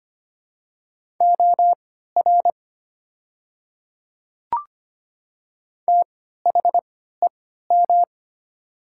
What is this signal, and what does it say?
Morse code, heard as a keyed tone, about 25 wpm, spaced as for 14 wpm, 700 Hz. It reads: OR THEM
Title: 3x Character Spacing - Top 200 Words - 25wpm